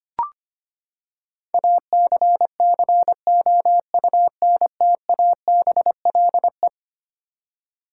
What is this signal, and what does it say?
Morse code, heard as a keyed tone, about 25 wpm, 700 Hz. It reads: ACCOUNTABLE
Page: morse-code-ninja-practice-25wpm